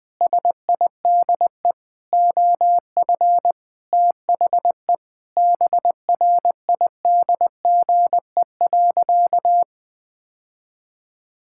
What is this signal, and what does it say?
Morse code, heard as a keyed tone, about 21 wpm, 700 Hz. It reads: SIDE OF THE BRIDGE.
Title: Sentences with 6-Word Phrases - 20wpm